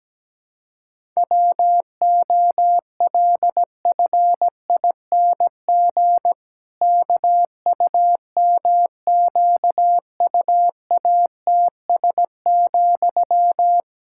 Morse code, heard as a keyed tone, about 17 wpm, 700 Hz. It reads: WOLFING KUMQUATS,
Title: Pangram Challenge v2 - 17wpm